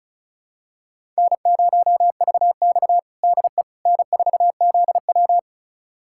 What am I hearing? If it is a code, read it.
Morse code, heard as a keyed tone, about 36 wpm, 700 Hz. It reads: N0VX DE N4ZW
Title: USA Callsign DE Callsign - 4 Characters - 35wpm